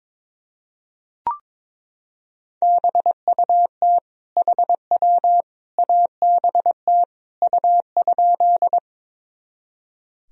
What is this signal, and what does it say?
Morse code, heard as a keyed tone, about 21 wpm, 700 Hz. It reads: BUT HW ABT U?
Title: Sentences from Top 100 Words and Abbreviations 22wpm